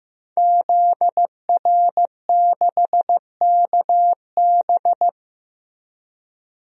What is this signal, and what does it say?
Morse code, heard as a keyed tone, about 16 wpm, 700 Hz. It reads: ZR6KB
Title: DXCC-100 Call Signs - 5 Characters - Repeated 2x - 15wpm